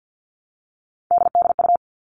Morse code, heard as a keyed tone, about 50 wpm, 700 Hz. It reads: DDU
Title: All Cut Numbers - 3-Digit 50wpm